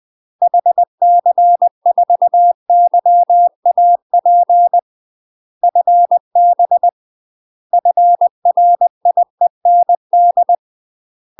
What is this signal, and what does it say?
Morse code, heard as a keyed tone, about 20 wpm, 700 Hz. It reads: HC4YAP FB FRIEND
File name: UK MASTER QSOs TRAINING / 20WPM / UK MASTER QSOs 0023